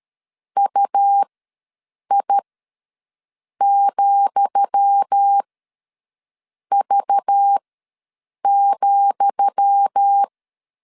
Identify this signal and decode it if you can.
Morse code, heard as a keyed tone, about 13 wpm, spaced as for 6 wpm, 800 Hz. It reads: UI,V,
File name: LEZIONI / 10 SIGNS / 10D COMA ws 12wpm 3618